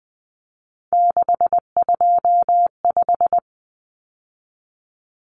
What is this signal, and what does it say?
Morse code, heard as a keyed tone, about 20 wpm, 700 Hz. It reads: 625
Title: Numbers - 3-Digit 20wpm